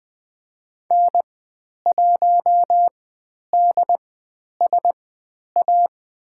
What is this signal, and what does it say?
Morse code, heard as a keyed tone, about 20 wpm, spaced as for 10 wpm, 700 Hz. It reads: N1DSA
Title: 095 Lesson 16g - Letter c - Call Signs 20wpm